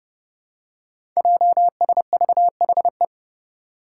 Morse code, heard as a keyed tone, about 30 wpm, 700 Hz. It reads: JSVHE
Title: Five Challenging Letter-Number Combinations 30wpm